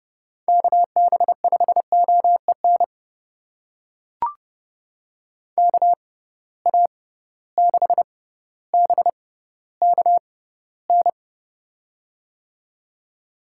Morse code, heard as a keyed tone, about 30 wpm, spaced as for 15 wpm, 700 Hz. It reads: KB5OEN KA6BKN
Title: USA Call Signs - 6 Characters - 6x Character Spacing - 30wpm